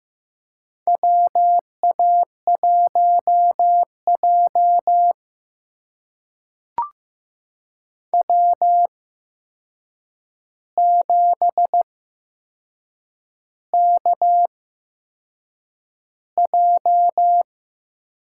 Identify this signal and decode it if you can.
Morse code, heard as a keyed tone, about 15 wpm, spaced as for 6 wpm, 700 Hz. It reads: WA1J W7KJ
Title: USA Call Signs - 4 Characters - 8x Character Spacing - 15wpm